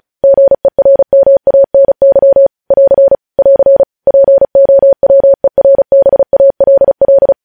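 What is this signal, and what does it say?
Morse code, heard as a keyed tone, about 36 wpm, 550 Hz. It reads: GERMANY <AR> <AR> POWERBALL
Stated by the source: News headlines sent in Morse at 35 wpm.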